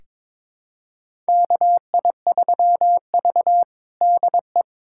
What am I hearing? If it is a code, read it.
Morse code, heard as a keyed tone, about 22 wpm, 700 Hz. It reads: KI3V DE
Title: USA Callsign DE Callsign 22wpm